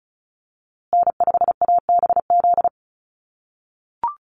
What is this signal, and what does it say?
Morse code, heard as a keyed tone, about 34 wpm, 700 Hz. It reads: N5ABZ
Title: USA Call Signs Repeated 2x 35wpm